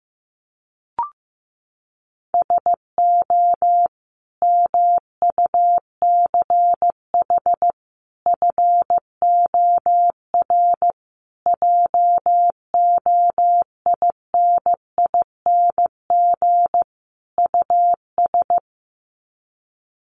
Morse code, heard as a keyed tone, about 15 wpm, 700 Hz. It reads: SO MUCH FOR JOINING US
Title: Sets of 5 Words 15wpm